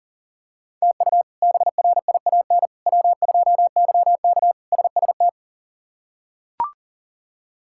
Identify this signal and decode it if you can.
Morse code, heard as a keyed tone, about 40 wpm, 700 Hz. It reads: TU BRIAN W2YK SST
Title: K1USN SST QSO Practice - 40wpm